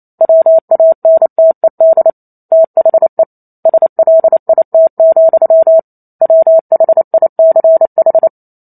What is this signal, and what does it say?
Morse code, heard as a keyed tone, about 28 wpm, 650 Hz. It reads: WANTED THE SLIT, WHICH